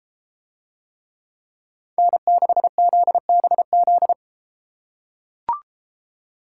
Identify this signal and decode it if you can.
Morse code, heard as a keyed tone, about 32 wpm, 700 Hz. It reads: N6ZBZ